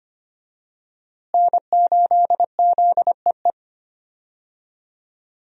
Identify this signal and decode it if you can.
Morse code, heard as a keyed tone, about 25 wpm, 700 Hz. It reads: N8ZEE
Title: USA Call Signs - 5 Characters - Repeated 3x - 25wpm